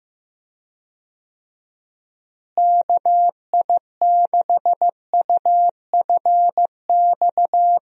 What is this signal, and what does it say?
Morse code, heard as a keyed tone, about 15 wpm, 700 Hz. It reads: KI6UFX
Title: USA Call Signs - 6 Characters - Repeated 2x - 15wpm